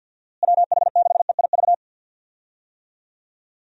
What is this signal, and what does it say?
Morse code, heard as a keyed tone, about 51 wpm, 700 Hz. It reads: WH6EIV
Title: USA Call Signs - 6 Characters - Repeated 3x - 50wpm